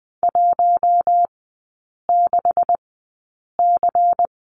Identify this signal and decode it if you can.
Morse code, heard as a keyed tone, about 20 wpm, spaced as for 8 wpm, 700 Hz. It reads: 16C
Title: 152 Lesson 25e - Letter p - Three Characters 20wpm